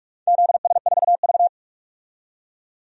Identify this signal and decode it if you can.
Morse code, heard as a keyed tone, about 42 wpm, 700 Hz. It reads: ZS4V
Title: DXCC-100 Call Signs - 4 Characters - 45wpm